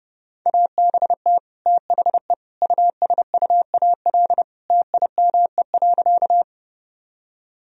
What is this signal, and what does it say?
Morse code, heard as a keyed tone, about 30 wpm, 700 Hz. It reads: ABT THE USUAL TIME.
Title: Sentences from Top 400 Words and Abbreviations 30wpm